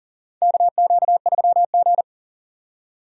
Morse code, heard as a keyed tone, about 40 wpm, 700 Hz. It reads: KQ3G
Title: USA Call Signs - 4 Characters - 40wpm